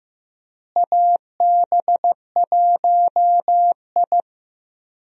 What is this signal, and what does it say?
Morse code, heard as a keyed tone, about 15 wpm, 700 Hz. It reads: AB1I
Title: USA Call Signs - 4 Characters - Repeated 3x - 15wpm